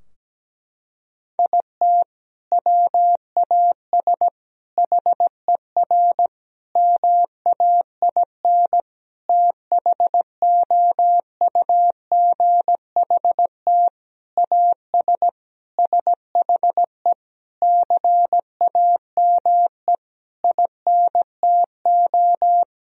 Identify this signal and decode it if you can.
Morse code, heard as a keyed tone, about 17 wpm, 700 Hz. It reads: IT WAS HER MAIN THOUGHT AS SHE CAME INTO
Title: Sentences from Top 1000 Words - Review Part 2 - 17wpm